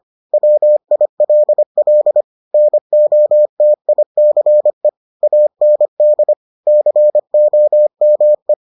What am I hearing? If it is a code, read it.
Morse code, heard as a keyed tone, about 25 wpm, 600 Hz. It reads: WILL NOTICE AND COME